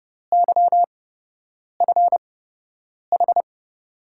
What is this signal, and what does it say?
Morse code, heard as a keyed tone, about 30 wpm, spaced as for 8 wpm, 700 Hz. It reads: YFH
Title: USA Call Signs - 5 Characters - 8x Character Spacing - 30wpm